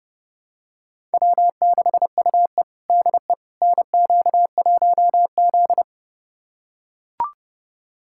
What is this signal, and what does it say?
Morse code, heard as a keyed tone, about 32 wpm, 700 Hz. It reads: W6UE DE NQ1Z
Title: USA Callsign DE Callsign - 4 Characters - 30wpm